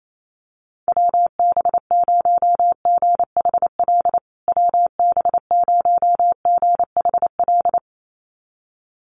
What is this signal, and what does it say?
Morse code, heard as a keyed tone, about 28 wpm, 700 Hz. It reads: WB0GHL WB0GHL
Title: USA Call Signs Repeated 2x 28wpm